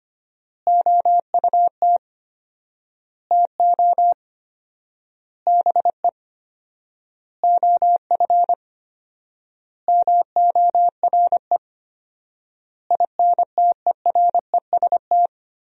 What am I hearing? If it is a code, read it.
Morse code, heard as a keyed tone, about 25 wpm, 700 Hz. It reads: OUT TO BE OF MORE INTEREST
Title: 4x Word-Spacing - Sentences from Top 400 Words - 25wpm